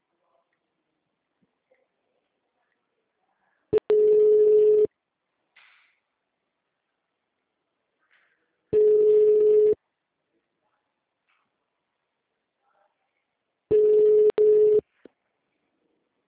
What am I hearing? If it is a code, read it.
Ringback tone (Europe, 425 Hz).
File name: out- -2017- - - .75